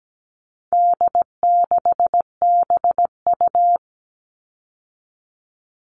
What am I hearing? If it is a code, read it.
Morse code, heard as a keyed tone, about 17 wpm, 700 Hz. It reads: D6BU
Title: All Cut Numbers - 4-Digit 17wpm